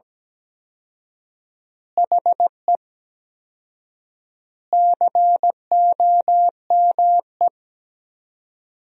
Morse code, heard as a keyed tone, about 17 wpm, 700 Hz. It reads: HE COME
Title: 4x Word-Spacing - Sentences from Top 300 Words - 17wpm